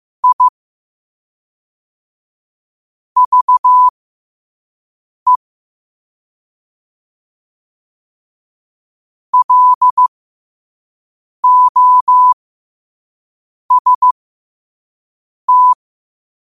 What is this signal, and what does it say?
Morse code, heard as a keyed tone, about 15 wpm, spaced as for 5 wpm, 1000 Hz. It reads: I VE LOST
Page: New quotes every day in morse code at 5 Words per minute.